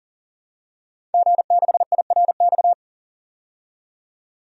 Morse code, heard as a keyed tone, about 40 wpm, 700 Hz. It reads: G6IRX
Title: DXCC-100 Call Signs - 5 Characters - 40wpm